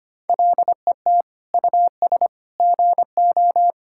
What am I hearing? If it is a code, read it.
Morse code, heard as a keyed tone, about 25 wpm, 700 Hz. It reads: LET US GO
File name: practice / 25wpm / Sentences with 4-Word Phrases - 25wpm